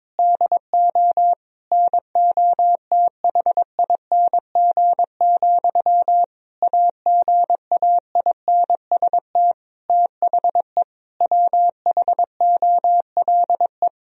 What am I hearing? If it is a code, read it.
Morse code, heard as a keyed tone, about 22 wpm, 700 Hz. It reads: DO NOTHING, AGAINST THE WHOLE
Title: Sentences from Top 1000 Words - Review Part 4 - 22wpm